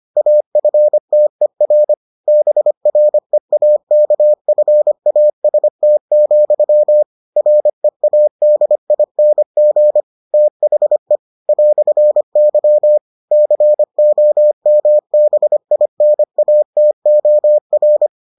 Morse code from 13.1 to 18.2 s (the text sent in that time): COMBINATOR